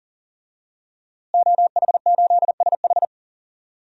Morse code, heard as a keyed tone, about 40 wpm, 700 Hz. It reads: OH8SH
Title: DXCC-100 Call Signs - 5 Characters - Repeated 3x - 40wpm